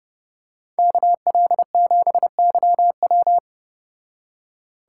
Morse code, heard as a keyed tone, about 29 wpm, 700 Hz. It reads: KL7YW